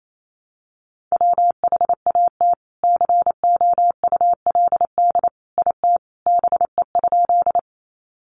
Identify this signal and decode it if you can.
Morse code, heard as a keyed tone, about 28 wpm, 700 Hz. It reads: WHAT COULD IT BE?
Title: Sentences from Top 100 Words 28wpm